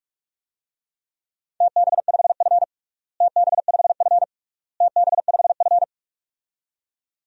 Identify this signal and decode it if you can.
Morse code, heard as a keyed tone, about 46 wpm, 690 Hz. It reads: TB5F TB5F TB5F
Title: DXCC-100 Call Signs - 4 Characters - Repeated 3x - 45wpm